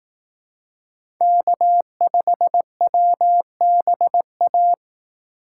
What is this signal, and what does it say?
Morse code, heard as a keyed tone, about 18 wpm, 700 Hz. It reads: K5WBA
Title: Speed Racing - USA 1x3 Call Signs 15wpm